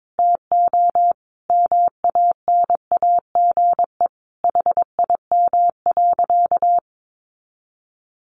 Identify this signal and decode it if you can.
Morse code, heard as a keyed tone, about 22 wpm, 700 Hz. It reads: TO MANAGE HIM.
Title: Sentences from Top 1400 Words 22wpm